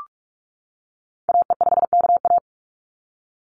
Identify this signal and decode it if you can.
Morse code, heard as a keyed tone, about 44 wpm, 700 Hz. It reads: AE5KA
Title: NA SOTA Activator Callsigns 45wpm